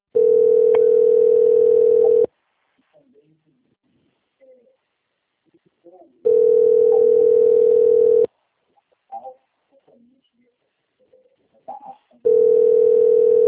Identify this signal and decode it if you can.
Ringback tone (North America), 440 + 480 Hz.